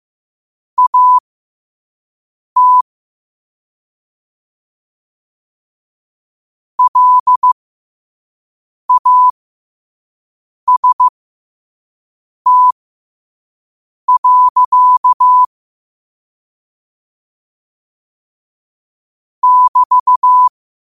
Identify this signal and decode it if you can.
Morse code, heard as a keyed tone, about 15 wpm, spaced as for 5 wpm, 1000 Hz. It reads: AT LAST. <BT>
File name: qotdfiles / 5wpm